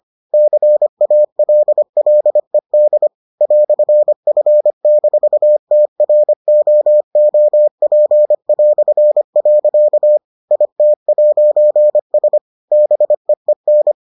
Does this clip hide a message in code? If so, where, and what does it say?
Yes: Morse code from 0.3 to 13.9 s, heard as a keyed tone, about 25 wpm, 600 Hz. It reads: CALLED "F-TROOP". IT'S BEEN